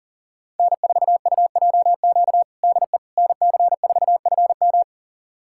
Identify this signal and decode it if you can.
Morse code, heard as a keyed tone, about 40 wpm, 700 Hz. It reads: N4UJQ DE NC4FM
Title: USA Callsign DE Callsign - 5 Characters - 40wpm